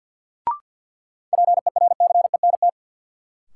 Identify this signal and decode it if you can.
Morse code, heard as a keyed tone, about 47 wpm, 700 Hz. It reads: PERCENT
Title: Sets of 2 Words - New Word Drill - 50wpm